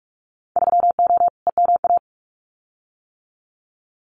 Morse code, heard as a keyed tone, about 46 wpm, 700 Hz. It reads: 3O EMA
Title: Winter Field Day Exchanges Encore 45wpm